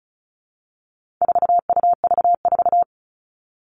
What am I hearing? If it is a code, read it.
Morse code, heard as a keyed tone, about 34 wpm, 700 Hz. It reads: 4UV4
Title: All Cut Numbers - 4-Digit 35wpm